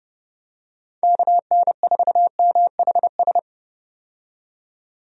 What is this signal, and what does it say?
Morse code, heard as a keyed tone, about 29 wpm, 700 Hz. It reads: KN4MHS
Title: USA Call Signs - 6 Characters - 30wpm